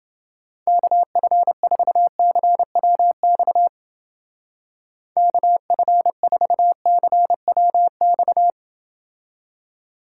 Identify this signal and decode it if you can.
Morse code, heard as a keyed tone, about 30 wpm, 700 Hz. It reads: KF4CWX KF4CWX